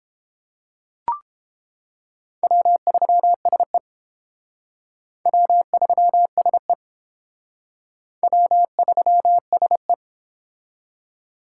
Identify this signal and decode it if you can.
Morse code, heard as a keyed tone, about 30 wpm, 700 Hz. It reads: W3SE W3SE W3SE